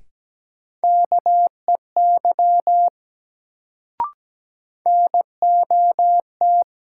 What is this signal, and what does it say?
Morse code, heard as a keyed tone, about 17 wpm, 700 Hz. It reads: KEY NOT
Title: Sentences from Top 600 Words - New Word Drill - 17wpm